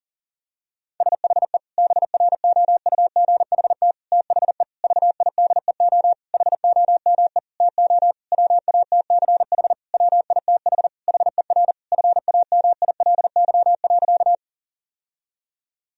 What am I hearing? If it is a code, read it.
Morse code, heard as a keyed tone, about 40 wpm, 700 Hz. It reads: SHE BROUGHT THE VIDEO HOME TO WATCH WITH HER FAMILY.
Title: Sentences from Top 1000 Words - Review Part 4 - 40wpm